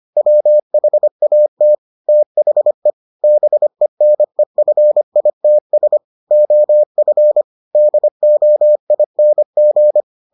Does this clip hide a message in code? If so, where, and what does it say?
Yes: Morse code from 0.2 to 10.0 s, heard as a keyed tone, about 25 wpm, 600 Hz. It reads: WHAT THE BENEFITS OF DOING